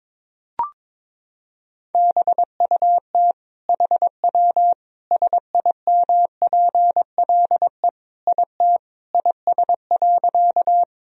Morse code, heard as a keyed tone, about 22 wpm, 700 Hz. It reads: BUT HW SIMPLE IT IS.
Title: Sentences from Top 400 Words and Abbreviations 22wpm